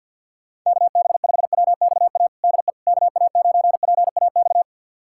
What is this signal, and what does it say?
Morse code, heard as a keyed tone, about 51 wpm, 700 Hz. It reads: KB5WXA DE KA9PAX
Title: USA Callsign DE Callsign - 6 Characters - 50wpm